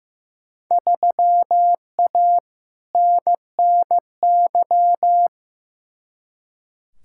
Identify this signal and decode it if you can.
Morse code, heard as a keyed tone, about 15 wpm, 700 Hz. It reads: 3A NNY
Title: Field Day Exchanges - 15wpm